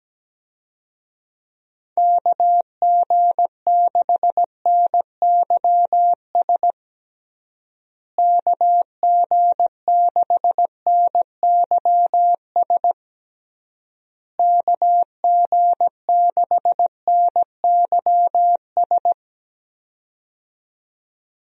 Morse code, heard as a keyed tone, about 17 wpm, 700 Hz. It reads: KG6NYS KG6NYS KG6NYS
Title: USA Call Signs - 6 Characters - Repeated 3x - 17wpm